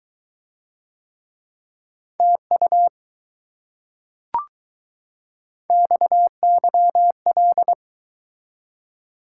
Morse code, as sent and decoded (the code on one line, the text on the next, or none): - ..- / -..- -.-- .-..
TU XYL